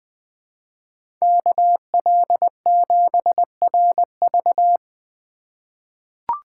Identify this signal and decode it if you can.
Morse code, heard as a keyed tone, about 20 wpm, 700 Hz. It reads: KL7RV
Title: USA Call Signs - 5 Characters - 4x Character Spacing - 20wpm